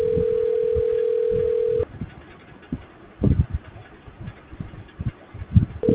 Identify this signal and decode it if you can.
Ringback tone (North America), 440 + 480 Hz.